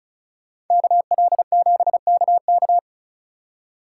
Morse code, heard as a keyed tone, about 43 wpm, 700 Hz. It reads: KL7KK
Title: Speed Racing - USA Call Signs 35wpm